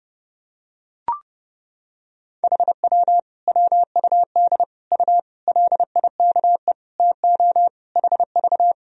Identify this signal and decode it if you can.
Morse code, heard as a keyed tone, about 30 wpm, 700 Hz. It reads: HW WUD U LIKE TO HV
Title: Sentences from Top 100 Words and Abbreviations 30wpm